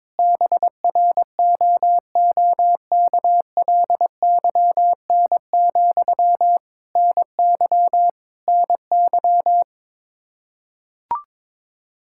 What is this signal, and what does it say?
Morse code, heard as a keyed tone, about 21 wpm, 700 Hz. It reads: BROOKLYN, NY NY
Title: Practice QSO - Exchange 1 - Beginning - 22wpm